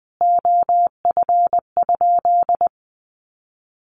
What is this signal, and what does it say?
Morse code, heard as a keyed tone, about 20 wpm, 700 Hz. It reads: OF?
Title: Sentences from Top 1400 Words 20wpm